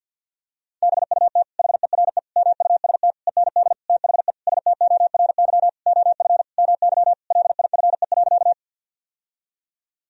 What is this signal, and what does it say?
Morse code, heard as a keyed tone, about 48 wpm, 700 Hz. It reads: BUT HERE MUST END THE STORY OF MY LIFE.
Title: Sentences from Top 1000 Words - Review Part 1 - 50wpm